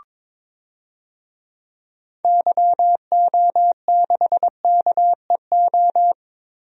Morse code, heard as a keyed tone, about 22 wpm, 700 Hz. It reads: YO6KEO
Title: DXCC-100 Call Signs - 6 Characters - 22wpm